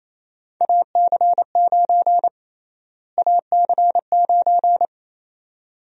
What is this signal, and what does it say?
Morse code, heard as a keyed tone, about 28 wpm, 700 Hz. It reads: AC9 AC9
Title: USA Call Sign Prefixes - Repeated 2x - 28wpm